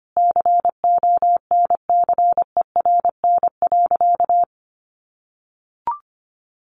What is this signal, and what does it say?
Morse code, heard as a keyed tone, about 25 wpm, 700 Hz. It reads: CONCERN.
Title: Sentences from Top 1400 Words 25wpm